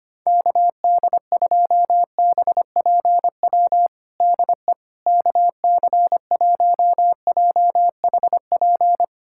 Morse code, heard as a keyed tone, about 25 wpm, 700 Hz. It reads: KD2BPW DE KC1JHP
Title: USA Callsign DE Callsign 25wpm